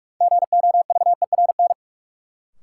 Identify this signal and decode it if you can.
Morse code, heard as a keyed tone, about 42 wpm, 700 Hz. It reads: GOVERN
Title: Sentences from Top 1000 Words - Review Part 2 - New Word Drill - 45wpm